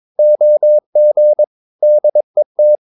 Morse code, heard as a keyed tone, about 22 wpm, 600 Hz. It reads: OG DET
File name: Ferie QRS 22wpm